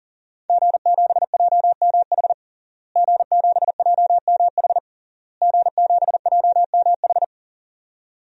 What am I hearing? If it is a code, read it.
Morse code, heard as a keyed tone, about 40 wpm, 700 Hz. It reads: G7JMH G7JMH G7JMH
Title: DXCC-100 Call Signs - 5 Characters - Repeated 3x - 40wpm